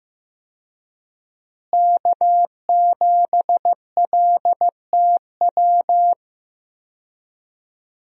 Morse code, heard as a keyed tone, about 15 wpm, 700 Hz. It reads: K7LTW
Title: USA Call Signs - 5 Characters - Repeated 2x - 15wpm